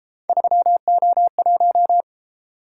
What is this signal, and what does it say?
Morse code, heard as a keyed tone, about 31 wpm, 700 Hz. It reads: 3O1